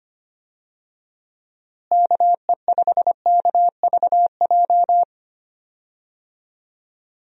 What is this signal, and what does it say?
Morse code, heard as a keyed tone, about 25 wpm, 700 Hz. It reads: KE5KVJ
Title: USA Call Signs - 6 Characters - Repeated 3x - 25wpm